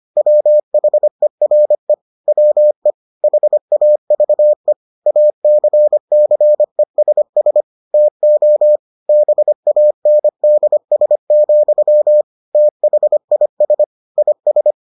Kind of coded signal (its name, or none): Morse code